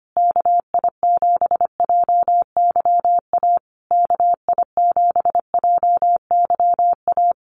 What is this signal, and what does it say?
Morse code, heard as a keyed tone, about 25 wpm, 700 Hz. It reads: KI7JYA KI7JYA
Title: USA Call Signs Repeated 3x 25wpm